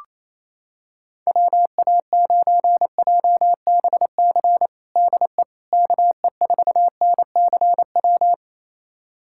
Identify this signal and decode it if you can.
Morse code, heard as a keyed tone, about 28 wpm, 700 Hz. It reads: WA9JBC DE KE4NCW
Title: USA Callsign DE Callsign - 6 Characters - 28wpm